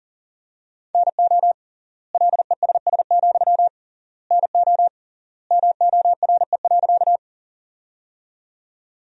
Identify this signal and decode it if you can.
Morse code, heard as a keyed tone, about 40 wpm, 700 Hz. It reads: NO LESS, NO MORE.
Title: 3x Word-Spacing - Sentences from Top 400 Words - 40wpm